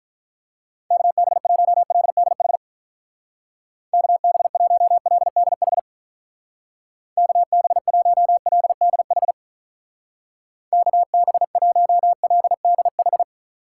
Morse code, heard as a keyed tone, about 45 wpm, 700 Hz. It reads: KB1LDH KB1LDH KB1LDH KB1LDH